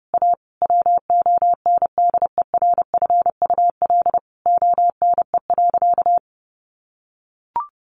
Morse code, heard as a keyed tone, about 28 wpm, 700 Hz. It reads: A WONDERFUL ONE.
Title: Sentences from Top 1100 Words 30wpm